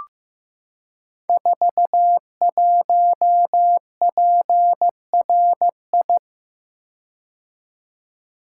Morse code, heard as a keyed tone, about 15 wpm, 700 Hz. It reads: 41PRI
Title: Code Groups - 5 Letter:Numbers - 15wpm